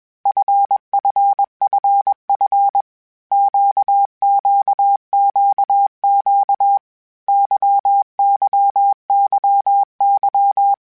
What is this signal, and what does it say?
Morse code, heard as a keyed tone, about 21 wpm, 800 Hz. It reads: FFFF QQQQ YYYY
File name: Basic Sets / Basic Letter Groups Collective7 / LFCYQ@20